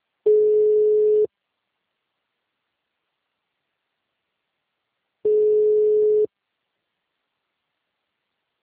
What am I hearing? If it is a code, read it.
Ringback tone (Europe, 425 Hz).